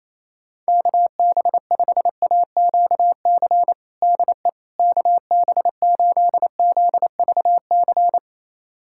Morse code, heard as a keyed tone, about 28 wpm, 700 Hz. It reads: KB5AQC DE KB8ZVC
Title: USA Callsign DE Callsign - 6 Characters - 28wpm